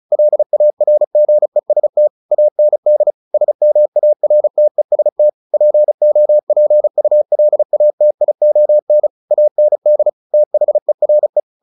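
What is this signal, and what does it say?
Morse code, heard as a keyed tone, about 34 wpm, 600 Hz. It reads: LARGEST AND SMARTEST POPULATION AND THERE
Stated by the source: Morse practice at 35 wpm from CW med Gnister, sending Ken Follett's novel Never in English.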